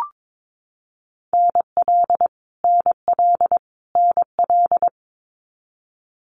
Morse code, heard as a keyed tone, about 22 wpm, 700 Hz. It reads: NL NL NL
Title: Canadian Provinces and Territories - Repeated 3x - 22wpm